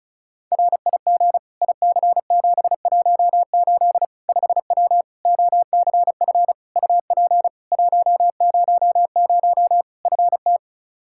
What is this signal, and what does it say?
Morse code, heard as a keyed tone, about 36 wpm, 700 Hz. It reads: RIG IC718 5W OCF UP 100 FT
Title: K1USN SST QSO Practice 35wpm